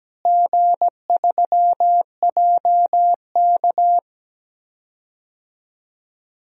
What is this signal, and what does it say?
Morse code, heard as a keyed tone, about 17 wpm, 700 Hz. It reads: G3JK
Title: DXCC-100 Call Signs - 4 Characters - Repeated 2x - 17wpm